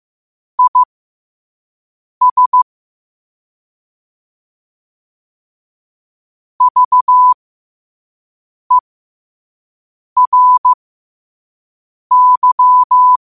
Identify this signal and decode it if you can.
Morse code, heard as a keyed tone, about 15 wpm, spaced as for 5 wpm, 1000 Hz. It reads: IS VERY
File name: qotdfiles / 5wpm